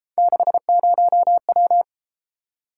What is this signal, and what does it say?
Morse code, heard as a keyed tone, about 30 wpm, 700 Hz. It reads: 60W